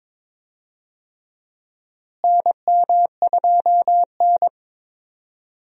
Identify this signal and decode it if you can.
Morse code, heard as a keyed tone, about 22 wpm, 700 Hz. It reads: NM2N
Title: USA Call Signs - 4 Characters - Repeated 3x - 22wpm